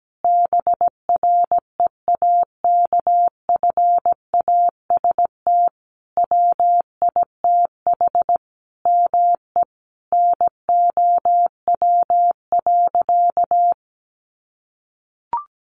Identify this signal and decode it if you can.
Morse code, heard as a keyed tone, about 17 wpm, 700 Hz. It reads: BREAKFAST WITH ME NOW.
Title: Sentences from Top 1500 Words 17wpm